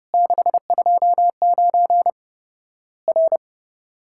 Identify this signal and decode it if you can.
Morse code, heard as a keyed tone, about 30 wpm, 700 Hz. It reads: 629 R
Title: Worked All Europe DX Contest - QTC Exchange - 30wpm